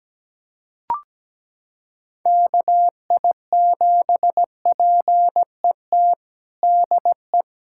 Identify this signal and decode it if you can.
Morse code, heard as a keyed tone, about 17 wpm, 700 Hz. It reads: KI7PET DE
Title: USA Callsign DE Callsign - 6 Characters - 17wpm